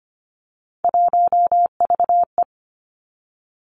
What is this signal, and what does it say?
Morse code, heard as a keyed tone, about 25 wpm, 700 Hz. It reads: 1VE
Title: Three Challenging Letter-Number Combinations 25wpm